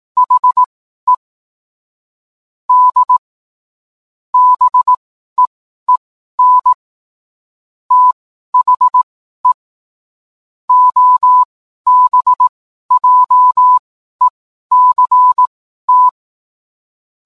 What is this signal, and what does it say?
Morse code, heard as a keyed tone, about 18 wpm, spaced as for 12 wpm, 1000 Hz. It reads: HE D BEEN THE OBJECT